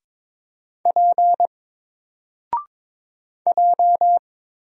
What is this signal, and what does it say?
Morse code, heard as a keyed tone, about 22 wpm, 700 Hz. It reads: P J
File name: Single-Letter-Number-22wpm